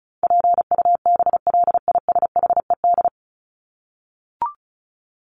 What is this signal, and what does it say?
Morse code, heard as a keyed tone, about 34 wpm, 700 Hz. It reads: PUBLISHED
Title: ED Suffix - Top 100 Words with Base Words 35wpm